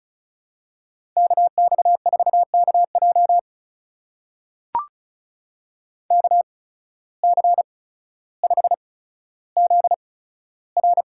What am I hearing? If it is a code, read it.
Morse code, heard as a keyed tone, about 34 wpm, spaced as for 15 wpm, 700 Hz. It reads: KX4KJ KC5ZR
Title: USA Call Signs - 5 Characters - 8x Character Spacing - 35wpm